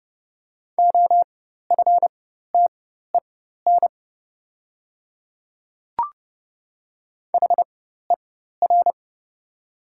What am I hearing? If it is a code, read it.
Morse code, heard as a keyed tone, about 30 wpm, spaced as for 14 wpm, 700 Hz. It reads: OFTEN HER
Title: 4x Character Spacing - Top 300 Words - 30wpm